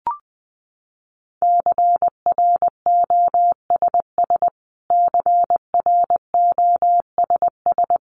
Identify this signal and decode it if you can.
Morse code, heard as a keyed tone, about 20 wpm, 700 Hz. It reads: CROSS CROSS
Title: Top 500 Words Repeated 20wpm